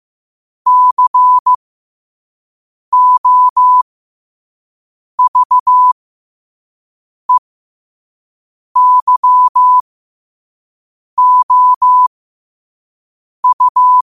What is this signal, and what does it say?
Morse code, heard as a keyed tone, about 15 wpm, spaced as for 5 wpm, 1000 Hz. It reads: COVEYOU